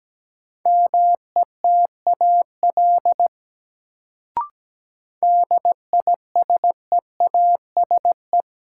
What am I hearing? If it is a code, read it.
Morse code, heard as a keyed tone, about 17 wpm, 700 Hz. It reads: METAL DISEASE
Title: Sentences from Top 1000 Words - Review Part 6 - New Word Drill - 17wpm